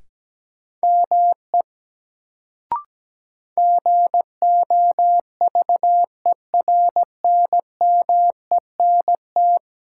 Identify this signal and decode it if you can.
Morse code, heard as a keyed tone, about 17 wpm, 700 Hz. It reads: ME GOVERNMENT
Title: Sets of 2 Words - New Word Drill - 17wpm